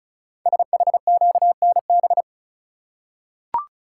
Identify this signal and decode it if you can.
Morse code, heard as a keyed tone, about 36 wpm, 690 Hz. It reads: SHQNB
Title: Code Groups - 5 Letters - 35wpm